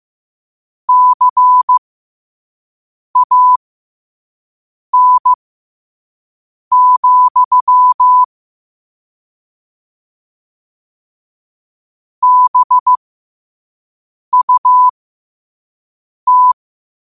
Morse code, heard as a keyed tone, about 15 wpm, spaced as for 5 wpm, 1000 Hz. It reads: CAN, BUT
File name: qotdfiles / 5wpm